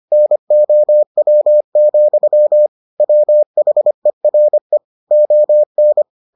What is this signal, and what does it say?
Morse code, heard as a keyed tone, about 25 wpm, 600 Hz. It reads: NOW, WHERE ON